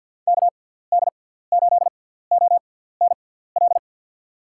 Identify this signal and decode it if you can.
Morse code, heard as a keyed tone, about 53 wpm, spaced as for 17 wpm, 700 Hz. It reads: KD8ONL
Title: USA Call Signs - 6 Characters - 6x Character Spacing - 50wpm